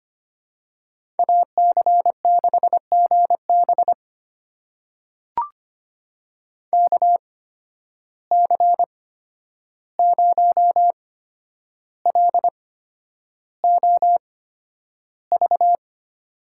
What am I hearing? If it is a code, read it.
Morse code, heard as a keyed tone, about 25 wpm, spaced as for 10 wpm, 700 Hz. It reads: AC6GB KC0LOV
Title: USA Call Signs - 8x Character Spacing - 25wpm